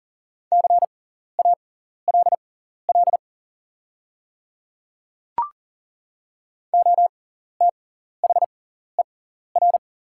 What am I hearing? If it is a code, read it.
Morse code, heard as a keyed tone, about 39 wpm, spaced as for 14 wpm, 700 Hz. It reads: CALL OTHER
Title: 6x Character Spacing - Top 200 Words - 40wpm